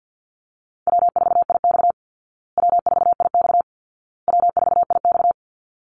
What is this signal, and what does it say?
Morse code, heard as a keyed tone, about 50 wpm, 700 Hz. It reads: W4IX W4IX W4IX
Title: Easy Call Signs Repeated 3x - 50wpm